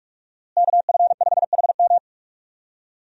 Morse code, heard as a keyed tone, about 46 wpm, 700 Hz. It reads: KF5HM
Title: USA Call Signs - 5 Characters - Repeated 3x - 45wpm